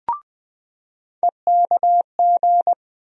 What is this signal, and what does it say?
Morse code, heard as a keyed tone, about 20 wpm, 700 Hz. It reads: EKG
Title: USA Call Sign Postfixes - 3 Letters 20wpm